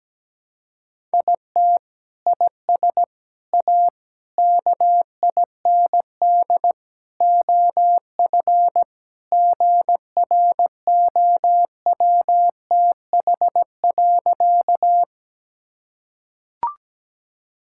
Morse code, heard as a keyed tone, about 17 wpm, 700 Hz. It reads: IT IS A KIND OF GROWTH.
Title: Sentences from Top 1000 Words - Review Part 5 - 17wpm